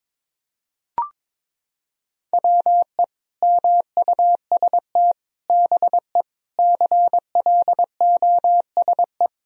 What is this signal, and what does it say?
Morse code, heard as a keyed tone, about 22 wpm, 700 Hz. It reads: WE MUST BE CLOSE
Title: Sentences from Top 1000 Words - Review Part 1 - 22wpm